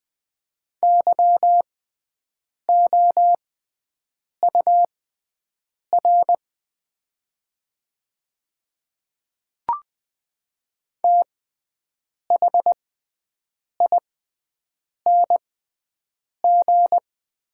Morse code, heard as a keyed tone, about 20 wpm, spaced as for 7 wpm, 700 Hz. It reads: YOUR THING
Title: 6x Character Spacing - Top 300 Words - 20wpm